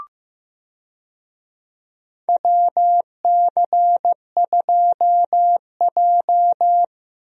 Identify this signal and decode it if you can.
Morse code, heard as a keyed tone, about 15 wpm, 700 Hz. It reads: WC2J
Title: USA Call Signs - 4 Characters - Repeated 3x - 15wpm